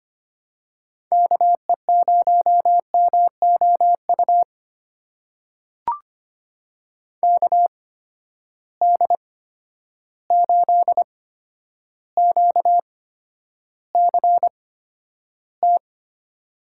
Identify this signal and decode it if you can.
Morse code, heard as a keyed tone, about 25 wpm, spaced as for 11 wpm, 700 Hz. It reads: KE0MOU KD8QCT
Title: USA Call Signs - 6 Characters - 8x Character Spacing - 25wpm